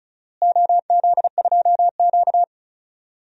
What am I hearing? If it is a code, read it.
Morse code, heard as a keyed tone, about 34 wpm, 700 Hz. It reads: OZ2Q
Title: DXCC-100 Call Signs - 4 Characters - Repeated 2x - 35wpm